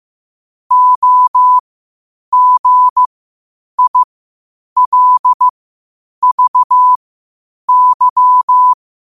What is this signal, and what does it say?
Morse code, heard as a keyed tone, about 15 wpm, spaced as for 8 wpm, 1000 Hz. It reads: OGILVY